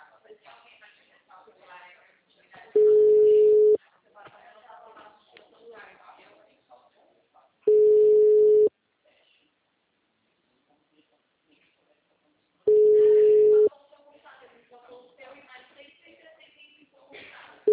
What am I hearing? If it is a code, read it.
Ringback tone (Europe, 425 Hz), with speech.